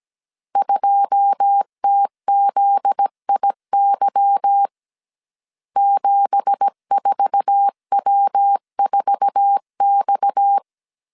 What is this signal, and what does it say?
Morse code, heard as a keyed tone, about 17 wpm, 800 Hz. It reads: 2TZIY 74W4X
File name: CW / DET / 1 9 1500C S A 16wpm 2828